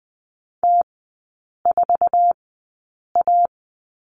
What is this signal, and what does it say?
Morse code, heard as a keyed tone, about 20 wpm, spaced as for 8 wpm, 690 Hz. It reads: T4A
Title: 133 Lesson 22e - Question Mark - Three Characters 20wpm